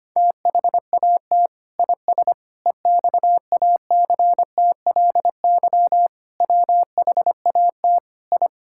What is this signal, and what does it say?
Morse code, heard as a keyed tone, about 25 wpm, 700 Hz. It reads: THAT IS EXACTLY WHAT I
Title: Sentences from Top 1000 Words - Review Part 6 - 25wpm